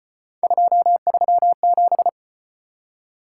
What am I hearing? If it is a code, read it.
Morse code, heard as a keyed tone, about 35 wpm, 700 Hz. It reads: 237